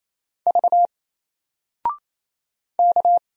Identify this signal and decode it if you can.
Morse code, heard as a keyed tone, about 28 wpm, 700 Hz. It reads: V K
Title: Single Character (Letter:Number:Punctuation:Prosign) 28wpm